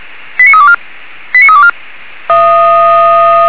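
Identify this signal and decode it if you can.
Five-tone selective call (ZVEI), heard as quick runs of five tones, tones of about 70 ms. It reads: 89324 89324
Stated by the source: ZVEI five-tone selective call received by SDR.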